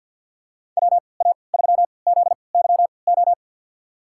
Morse code, heard as a keyed tone, about 53 wpm, spaced as for 29 wpm, 700 Hz. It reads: WA3ZYO
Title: USA Call Signs - 6 Characters - 3x Character Spacing - 50wpm